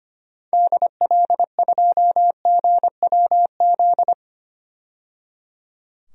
Morse code, heard as a keyed tone, about 25 wpm, 700 Hz. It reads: DL2GWZ
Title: DXCC-100 Call Signs - 6 Characters - 25wpm